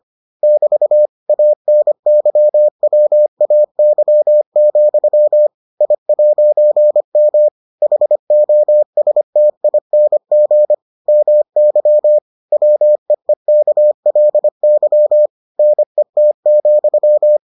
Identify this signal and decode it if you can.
Morse code, heard as a keyed tone, about 25 wpm, 600 Hz. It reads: <BT> ANYWAY, I'M HOSTING MY WEEKLY NET,